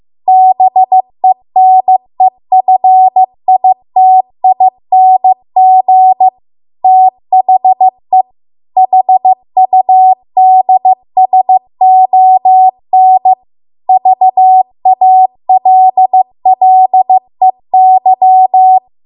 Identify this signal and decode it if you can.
Morse code, heard as a keyed tone, about 15 wpm, 750 Hz. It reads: BENEFITING THE HUDSON VALLEY